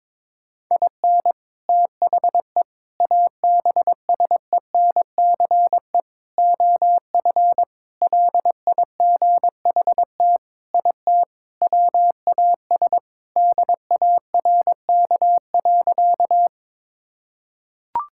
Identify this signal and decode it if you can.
Morse code, heard as a keyed tone, about 22 wpm, 700 Hz. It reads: IN THE ABSENCE OF LIGHT IT WAS DARK.
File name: practice / 22wpm / Sentences with 4-Word Phrases - 22wpm